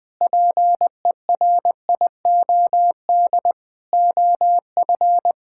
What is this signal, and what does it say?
Morse code, heard as a keyed tone, about 20 wpm, 700 Hz. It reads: PERIOD OF
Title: Sentences with 6-Word Phrases - 20wpm